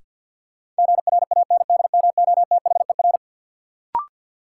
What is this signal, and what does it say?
Morse code, heard as a keyed tone, about 53 wpm, 700 Hz. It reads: GRANDMOTHER
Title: Sentences from Top 1200 Words - New Word Drill - 50wpm